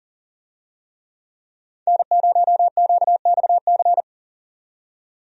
Morse code, heard as a keyed tone, about 40 wpm, 700 Hz. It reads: N0QXC
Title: USA Call Signs - 5 Characters - 40wpm